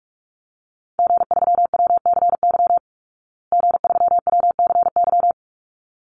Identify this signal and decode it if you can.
Morse code, heard as a keyed tone, about 44 wpm, 700 Hz. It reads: G3WCY G3WCY
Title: Call Signs Repeated 3x - 45wpm